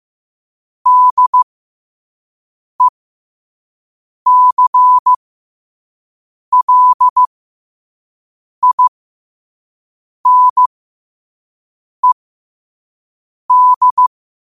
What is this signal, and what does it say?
Morse code, heard as a keyed tone, about 15 wpm, spaced as for 5 wpm, 1000 Hz. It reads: DECLINED
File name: qotdfiles / 5wpm